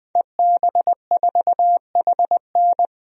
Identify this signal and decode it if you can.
Morse code, heard as a keyed tone, about 20 wpm, 700 Hz. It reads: EB4HN
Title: DXCC-100 Call Signs - 5 Characters - 20wpm